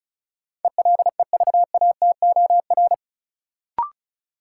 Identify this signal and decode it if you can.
Morse code, heard as a keyed tone, about 36 wpm, 700 Hz. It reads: ELEVATOR
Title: Sentences from Top 1600 Words - New Word Drill - 35wpm